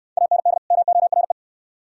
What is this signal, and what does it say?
Morse code, heard as a keyed tone, about 60 wpm, 700 Hz. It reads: AND MORE